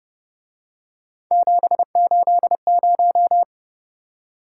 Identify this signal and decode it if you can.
Morse code, heard as a keyed tone, about 29 wpm, 700 Hz. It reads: 780
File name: Speed-Racing - Three Letter-Number Combinations - 25wpm